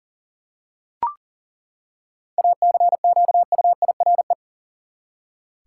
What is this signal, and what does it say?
Morse code, heard as a keyed tone, about 38 wpm, 700 Hz. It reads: ACQUIRE
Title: Seven Letter Words - 40wpm